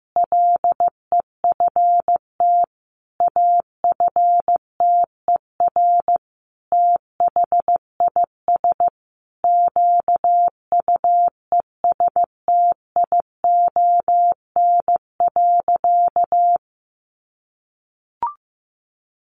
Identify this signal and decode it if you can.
Morse code, heard as a keyed tone, about 14 wpm, 700 Hz. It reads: LEFT AFTER THIS QUESTION.
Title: Sentences from Top 1500 Words 15wpm